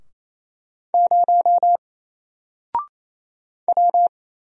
Morse code, heard as a keyed tone, about 27 wpm, 700 Hz. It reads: 0 W
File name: Single-Letter-Number-28wpm